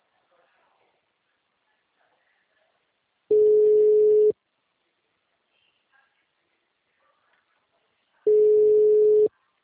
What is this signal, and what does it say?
Ringback tone (Europe, 425 Hz).
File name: out- -2029- - - .70992